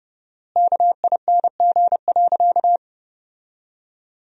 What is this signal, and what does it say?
Morse code, heard as a keyed tone, about 30 wpm, 700 Hz. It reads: KING.
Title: 4x Word-Spacing - Sentences from Top 400 Words - 30wpm